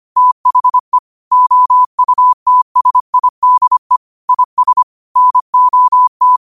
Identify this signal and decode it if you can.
Morse code, heard as a keyed tone, about 25 wpm, 1000 Hz. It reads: THE OUTSIDE IS NOT